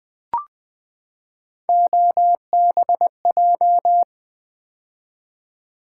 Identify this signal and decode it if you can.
Morse code, heard as a keyed tone, about 20 wpm, 700 Hz. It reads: OBJ
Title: Build A Word - 3 Letters At A Time - 9-Letter Words - 20wpm